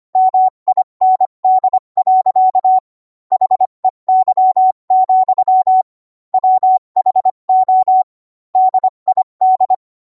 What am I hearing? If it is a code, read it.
Morse code, heard as a keyed tone, about 25 wpm, 750 Hz. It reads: MIND. HEY, WHO DID